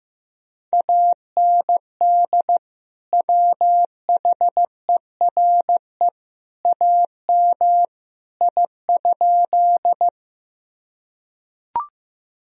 Morse code, heard as a keyed tone, about 15 wpm, 700 Hz. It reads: AND WHERE AM I?
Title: Sentences from Top 1000 Words - Review Part 2 - 15wpm